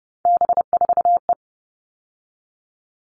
Morse code, heard as a keyed tone, about 30 wpm, 700 Hz. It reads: B4E
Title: All Cut Numbers - 3-Digit 30wpm